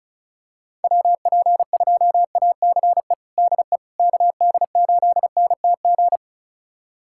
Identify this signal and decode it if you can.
Morse code, heard as a keyed tone, about 36 wpm, 690 Hz. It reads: WP2ACE DE KD8NTG
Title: USA Callsign DE Callsign - 6 Characters - 35wpm